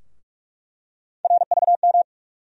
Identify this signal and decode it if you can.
Morse code, heard as a keyed tone, about 46 wpm, 700 Hz. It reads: RVM
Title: USA Call Sign Postfixes - 3 Letters 45wpm